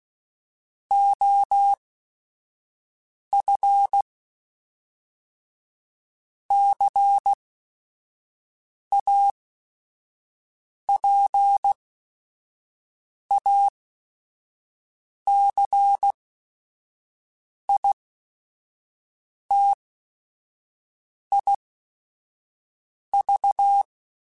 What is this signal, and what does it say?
Morse code, heard as a keyed tone, about 16 wpm, spaced as for 5 wpm, 800 Hz. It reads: OF CAPACITIV
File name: Technical Subjects / Technical Subjects-mp3s / XC @5